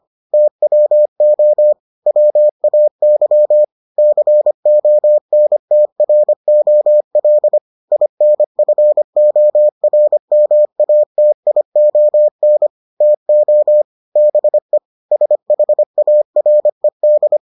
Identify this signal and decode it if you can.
Morse code, heard as a keyed tone, about 25 wpm, 600 Hz. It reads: TWO WAY CONTROL INFORMATION TO BE SHARED